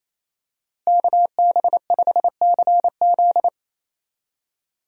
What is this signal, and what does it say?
Morse code, heard as a keyed tone, about 28 wpm, 700 Hz. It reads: KB5CZ
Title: USA Call Signs - 5 Characters - Repeated 2x - 28wpm